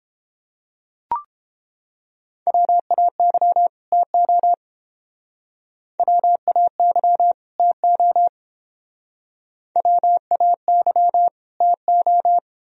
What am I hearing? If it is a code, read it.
Morse code, heard as a keyed tone, about 28 wpm, 700 Hz. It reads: WAY TO WAY TO WAY TO